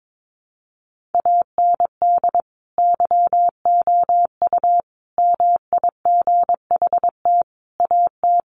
Morse code, heard as a keyed tone, about 22 wpm, 700 Hz. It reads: AND YOU MIGHT AT
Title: Sentences from Top 1500 Words 22wpm